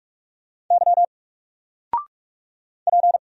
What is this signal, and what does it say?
Morse code, heard as a keyed tone, about 46 wpm, 700 Hz. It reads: Y P
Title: Single Letter-Number 45wpm